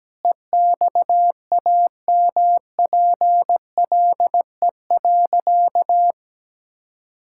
Morse code, heard as a keyed tone, about 17 wpm, 700 Hz. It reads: EXAMPLE.
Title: Sentences from Top 1000 Words - Review Part 5 - 17wpm